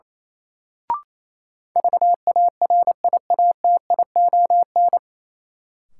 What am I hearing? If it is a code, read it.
Morse code, heard as a keyed tone, about 28 wpm, 700 Hz. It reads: VARIATION
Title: Sentences from Top 1600 Words - New Word Drill Alt - 28wpm